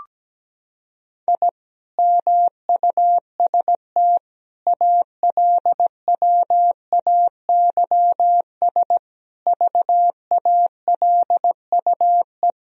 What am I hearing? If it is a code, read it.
Morse code, heard as a keyed tone, about 17 wpm, 700 Hz. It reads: I MUST ALWAYS VALUE
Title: Sentences from Top 1000 Words - Review Part 3 - 17wpm